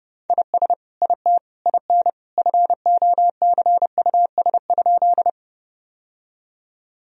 Morse code, heard as a keyed tone, about 30 wpm, 700 Hz. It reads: IS IT IN FOCUS?
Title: Sentences from Top 1000 Words - Review Part 4 - 30wpm